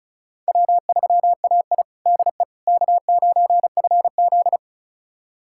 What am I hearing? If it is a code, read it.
Morse code, heard as a keyed tone, about 36 wpm, 700 Hz. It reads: W3AI DE K9FZ
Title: USA Callsign DE Callsign - 4 Characters - 35wpm